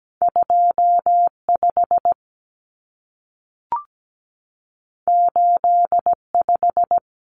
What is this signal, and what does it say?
Morse code, heard as a keyed tone, about 17 wpm, 700 Hz. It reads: 25 85
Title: Numbers - 2-Digit 17wpm